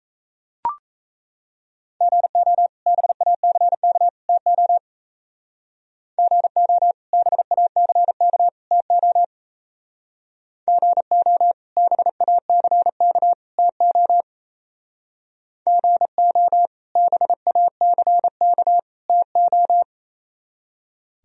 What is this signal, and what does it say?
Morse code, heard as a keyed tone, about 34 wpm, 700 Hz. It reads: GO BACK TO GO BACK TO GO BACK TO GO BACK TO
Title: Speed-Racing - Sets of 3 Words 28wpm